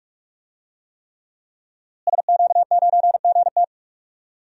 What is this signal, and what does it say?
Morse code, heard as a keyed tone, about 42 wpm, 700 Hz. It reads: SQ9GT
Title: DXCC-100 Call Signs - 5 Characters - Repeated 3x - 45wpm